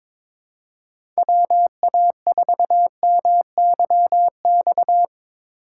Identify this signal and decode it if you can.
Morse code, heard as a keyed tone, about 22 wpm, 700 Hz. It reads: WA4MYX
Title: USA Call Signs - 6 Characters - 3x Character Spacing - 22wpm